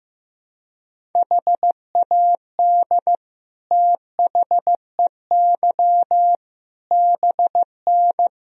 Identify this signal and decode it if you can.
Morse code, heard as a keyed tone, about 15 wpm, 700 Hz. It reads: HAD THEY BN
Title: Sentences from Top 400 Words and Abbreviations 15wpm